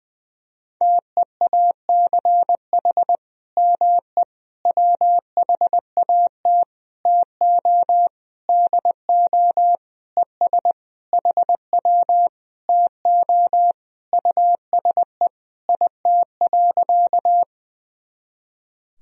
Morse code, heard as a keyed tone, about 21 wpm, 700 Hz. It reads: TEACH ME WHAT TO DO ES HW TO USE IT.
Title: Sentences from Top 500 Words and Abbreviations 20wpm